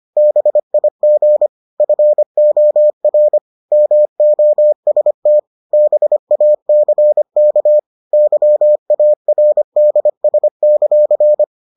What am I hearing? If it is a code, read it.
Morse code, heard as a keyed tone, about 25 wpm, 600 Hz. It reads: BIG FOR MOST BACK YARDS;